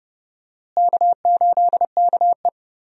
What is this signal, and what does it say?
Morse code, heard as a keyed tone, about 32 wpm, 700 Hz. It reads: K8KE
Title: USA Call Signs - 4 Characters - Repeated 3x - 30wpm